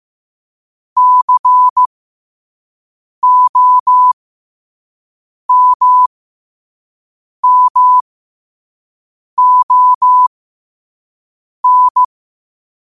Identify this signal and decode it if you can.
Morse code, heard as a keyed tone, about 15 wpm, spaced as for 5 wpm, 1000 Hz. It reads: COMMON
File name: qotdfiles / 5wpm